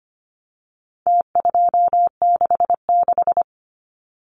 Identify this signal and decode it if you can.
Morse code, heard as a keyed tone, about 25 wpm, 700 Hz. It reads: T266
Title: Numbers - Contest Style - 4-Digit 25wpm